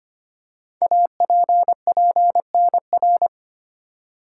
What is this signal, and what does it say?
Morse code, heard as a keyed tone, about 25 wpm, 700 Hz. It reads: APPNR
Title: Code Groups - 5 Letter:Numbers - 25wpm